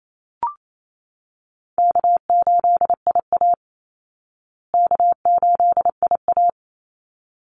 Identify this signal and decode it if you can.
Morse code, heard as a keyed tone, about 28 wpm, 700 Hz. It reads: K8IA K8IA
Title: Easy Call Signs Repeated 2x - 28wpm